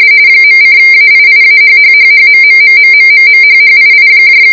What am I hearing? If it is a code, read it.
RTTY, about 45.45 baud. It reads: .:AIIAHHHHMMMMMMMHHII